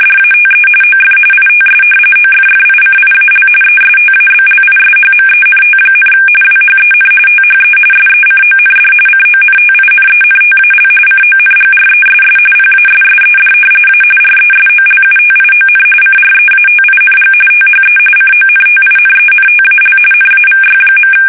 RTTY, about 45.45 baud. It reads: DATE OF ITS OFFICIAL ESTABLISHMENT, WHEN THE SECOND CONTINENTAL CONGRESS PASSED A RESOLUTION CREATING THE CONTINENTAL NAVY.
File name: fsk850 / usnavy1